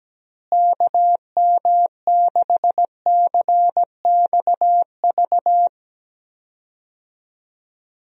Morse code, heard as a keyed tone, about 17 wpm, 700 Hz. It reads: KM6CXV
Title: USA Call Signs - 6 Characters - Repeated 3x - 17wpm